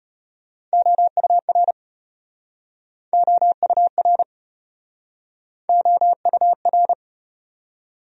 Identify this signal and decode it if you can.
Morse code, heard as a keyed tone, about 34 wpm, 700 Hz. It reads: OUR OUR OUR